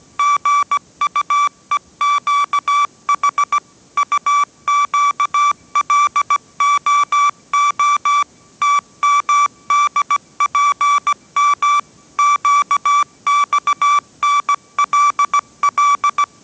Morse code, heard as a keyed tone, about 17 wpm, 1160 Hz. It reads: GUEQH UQLOO TMDPM QXNLL